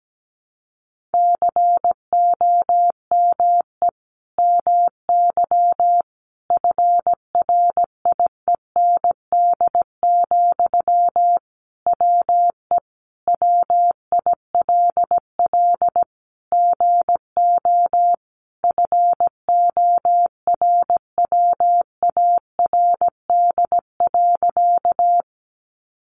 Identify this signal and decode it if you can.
Morse code, heard as a keyed tone, about 17 wpm, 700 Hz. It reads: COME MY FRIEND, WE WILL GO FORWARD.
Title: Sentences from Top 1300 Words 17wpm